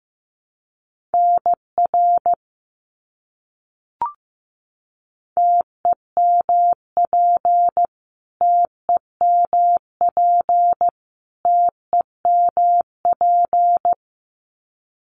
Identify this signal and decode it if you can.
Morse code, heard as a keyed tone, about 15 wpm, 700 Hz. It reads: NR TEMP TEMP TEMP
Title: Abbreviations Repeated 3x - 15wpm